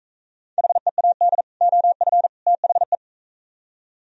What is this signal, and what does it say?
Morse code, heard as a keyed tone, about 43 wpm, 700 Hz. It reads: HEAD OF THE